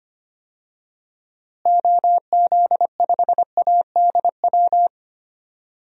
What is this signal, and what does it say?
Morse code, heard as a keyed tone, about 25 wpm, 700 Hz. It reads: OZ5ADW
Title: DXCC-100 Call Signs - 6 Characters - Repeated 3x - 25wpm